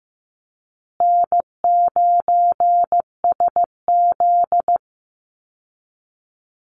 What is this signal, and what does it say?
Morse code, heard as a keyed tone, about 15 wpm, 700 Hz. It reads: N9SZ
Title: Easy Call Signs Repeated 2x - 15wpm